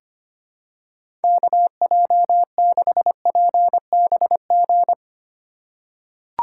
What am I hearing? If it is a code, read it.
Morse code, heard as a keyed tone, about 25 wpm, 700 Hz. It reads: KJ6PBG
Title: USA Call Signs - 2x Character Spacing - 25wpm